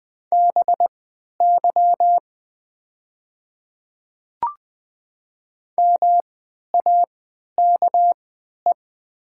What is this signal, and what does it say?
Morse code, heard as a keyed tone, about 20 wpm, spaced as for 11 wpm, 700 Hz. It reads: BY MAKE
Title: 3x Character Spacing - Top 100 Words - 20wpm